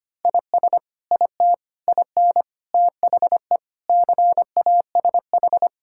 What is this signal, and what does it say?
Morse code, heard as a keyed tone, about 25 wpm, 700 Hz. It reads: IS IT IN THE CASH
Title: Sentences from Top 1000 Words - Review Part 5 - 25wpm